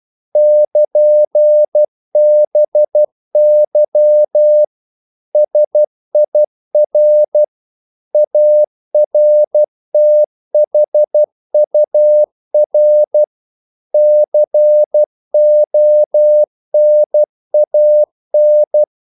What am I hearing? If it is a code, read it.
Morse code, heard as a keyed tone, about 12 wpm, 600 Hz. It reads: <KN>BY SIR ARTHUR CONAN
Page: English quotations in CW (telegraphy - morse)